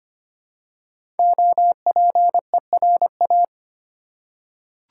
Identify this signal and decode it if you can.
Morse code, heard as a keyed tone, about 25 wpm, 700 Hz. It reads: OPERA
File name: 5 Letter Words - First 3 Letters Given - 25wpm